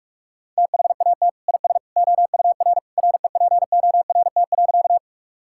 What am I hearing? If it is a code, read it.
Morse code, heard as a keyed tone, about 43 wpm, 700 Hz. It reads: THAT IS OUR REPORT.
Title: Sentences from Top 1000 Words - Review Part 6 - 45wpm